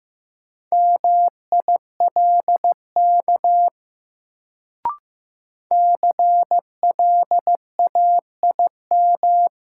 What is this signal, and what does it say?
Morse code, heard as a keyed tone, about 15 wpm, 700 Hz. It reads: MILK CLAIM
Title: Sentences from Top 1500 Words - New Word Drill - 15wpm